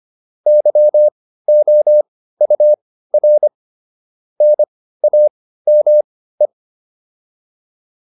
Morse code, heard as a keyed tone, about 25 wpm, spaced as for 15 wpm, 600 Hz. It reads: YOUR NAME
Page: intermediate-practice-files